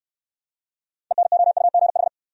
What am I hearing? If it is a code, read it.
Morse code, heard as a keyed tone, about 69 wpm, 700 Hz. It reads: ENOUGH